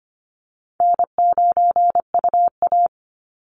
Morse code, heard as a keyed tone, about 25 wpm, 700 Hz. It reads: N9UA
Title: Easy Call Signs - 25wpm